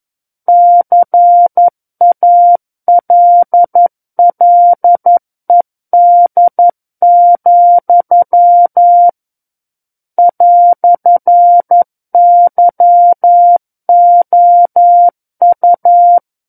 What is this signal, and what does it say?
Morse code, heard as a keyed tone, about 11 wpm, 700 Hz. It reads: CALLED, "YOU
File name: 11 WPM